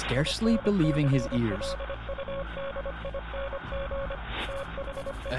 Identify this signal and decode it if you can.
Morse code, heard as a keyed tone, about 25 wpm, 550 Hz, with speech and radio noise. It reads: INCLUDING THE